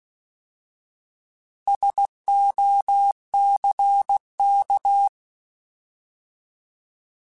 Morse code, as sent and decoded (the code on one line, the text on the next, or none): ... --- -.-. -.-
SOCK